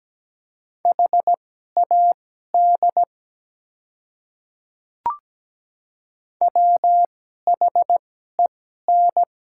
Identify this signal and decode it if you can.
Morse code, heard as a keyed tone, about 17 wpm, spaced as for 12 wpm, 700 Hz. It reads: HAD WHEN
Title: 2x Character Spacing - Top 100 Words - 17wpm